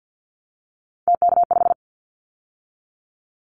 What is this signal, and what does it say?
Morse code, heard as a keyed tone, about 50 wpm, 700 Hz. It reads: TK5
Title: Callsign Prefixes for Top 100 DXCC - 50wpm